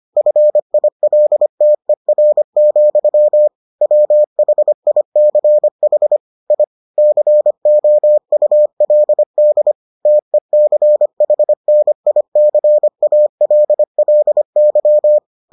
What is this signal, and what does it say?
Morse code, heard as a keyed tone, about 25 wpm, 600 Hz. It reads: FILTER, WHICH I COULD TECHNICALLY